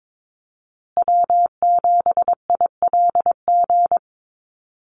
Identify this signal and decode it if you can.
Morse code, heard as a keyed tone, about 22 wpm, 700 Hz. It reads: W7ILG
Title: USA Call Signs Repeated 3x 22wpm